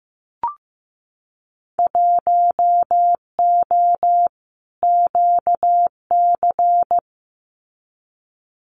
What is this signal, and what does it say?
Morse code, heard as a keyed tone, about 15 wpm, 700 Hz. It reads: 1O QC
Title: Winter Field Day Exchanges 15wpm